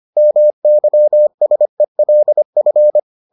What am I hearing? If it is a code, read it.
Morse code, heard as a keyed tone, about 25 wpm, 600 Hz. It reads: MYSELF